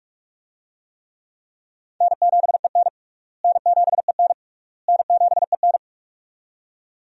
Morse code, heard as a keyed tone, about 46 wpm, 700 Hz. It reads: N7EN N7EN N7EN
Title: USA Call Signs - 4 Characters - Repeated 3x - 45wpm